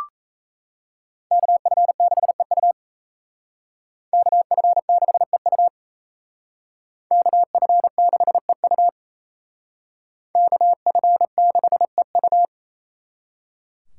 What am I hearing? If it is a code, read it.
Morse code, heard as a keyed tone, about 36 wpm, 700 Hz. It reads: KF6EU KF6EU KF6EU KF6EU